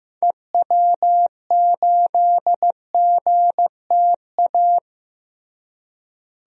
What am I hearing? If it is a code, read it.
Morse code, heard as a keyed tone, about 15 wpm, 700 Hz. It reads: EW8GTA
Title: DXCC-100 Call Signs - 6 Characters - Repeated 3x - 15wpm